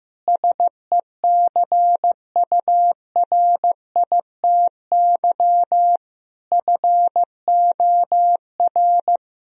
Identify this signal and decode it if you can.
Morse code, heard as a keyed tone, about 15 wpm, 700 Hz. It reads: SECURITY FOR
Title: Sentences from Top 1000 Words - Review Part 4 - 15wpm